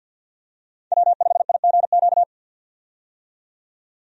Morse code, heard as a keyed tone, about 50 wpm, 700 Hz. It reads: W5IGQ
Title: USA Call Signs - 5 Characters - 50wpm